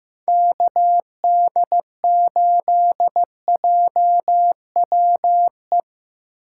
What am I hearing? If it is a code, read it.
Morse code, heard as a keyed tone, about 15 wpm, 700 Hz. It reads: KD8JWE
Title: USA Call Signs - 2x Character Spacing - 15wpm